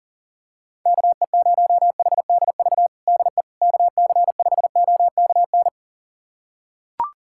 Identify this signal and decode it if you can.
Morse code, heard as a keyed tone, about 40 wpm, 700 Hz. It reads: KE0HDV DE KC5OKN
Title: USA Callsign DE Callsign 40wpm